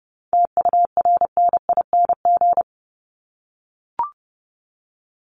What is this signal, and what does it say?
Morse code, heard as a keyed tone, about 30 wpm, 700 Hz. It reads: TURNING
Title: ING Suffix - Top 100 Words with Base Words 30wpm